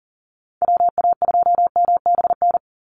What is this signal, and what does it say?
Morse code, heard as a keyed tone, about 40 wpm, 700 Hz. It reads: WA2MBN
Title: USA Call Signs 40wpm